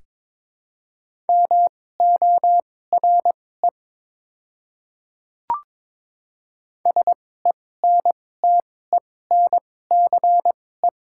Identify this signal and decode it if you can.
Morse code, heard as a keyed tone, about 22 wpm, spaced as for 16 wpm, 700 Hz. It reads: MORE SENTENCE
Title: 2x Character Spacing - Top 200 Words - 22wpm